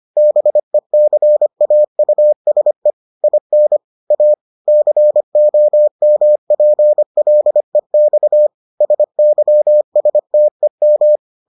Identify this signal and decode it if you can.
Morse code, heard as a keyed tone, about 25 wpm, 600 Hz. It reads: BECAUSE IN A COMPLEX SYSTEM